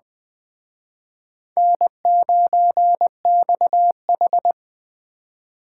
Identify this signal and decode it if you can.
Morse code, heard as a keyed tone, about 21 wpm, 700 Hz. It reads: N9XH
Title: Speed Racing - USA 1x2 Call Signs 17wpm